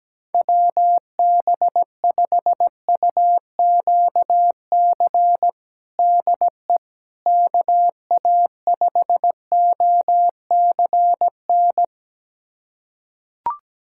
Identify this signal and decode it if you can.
Morse code, heard as a keyed tone, about 17 wpm, 700 Hz. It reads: WB5UQC DE KA5OCN
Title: USA Callsign DE Callsign - 6 Characters - 17wpm